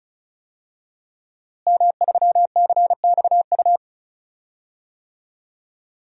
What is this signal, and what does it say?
Morse code, heard as a keyed tone, about 36 wpm, 700 Hz. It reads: M3CXU
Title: DXCC-100 Call Signs - 5 Characters - Repeated 3x - 35wpm